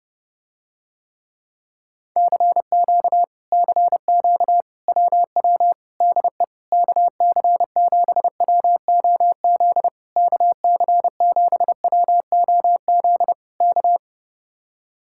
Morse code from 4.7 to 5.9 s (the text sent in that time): WW